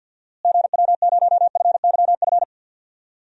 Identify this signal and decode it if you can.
Morse code, heard as a keyed tone, about 51 wpm, 700 Hz. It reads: GW0FYF
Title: DXCC-100 Call Signs - 6 Characters - Repeated 3x - 50wpm